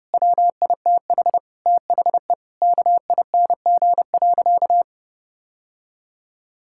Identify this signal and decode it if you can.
Morse code, heard as a keyed tone, about 30 wpm, 700 Hz. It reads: WITH THE KING.
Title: Sentences from Top 1000 Words - Review Part 6 - 30wpm